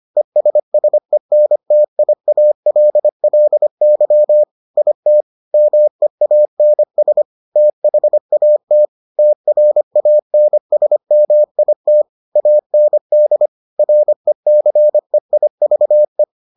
Morse code, heard as a keyed tone, about 25 wpm, 600 Hz. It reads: ESSENTIALLY IT MEANS THAT TRANSMIT AND RECEIVE